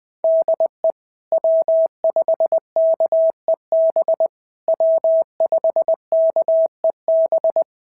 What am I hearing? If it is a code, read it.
Morse code, heard as a keyed tone, about 20 wpm, 650 Hz. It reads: DE W5KEB W5KEB
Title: POTA QSO Practice - 20wpm